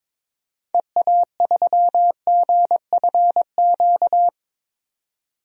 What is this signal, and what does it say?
Morse code, heard as a keyed tone, about 21 wpm, 700 Hz. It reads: EA3GFQ
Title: DXCC-100 Call Signs - 6 Characters - 22wpm